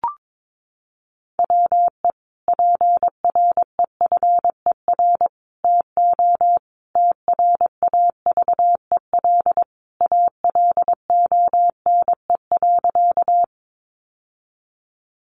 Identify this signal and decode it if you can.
Morse code, heard as a keyed tone, about 22 wpm, 700 Hz. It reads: WE PREFER TO TRAVEL ALONE.
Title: Sentences from Top 1100 Words 22wpm